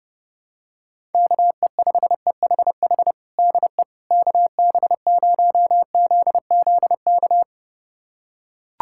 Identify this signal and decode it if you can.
Morse code, heard as a keyed tone, about 30 wpm, 700 Hz. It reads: KE5EHH DE KB0ZZK
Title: USA Callsign DE Callsign - 6 Characters - 30wpm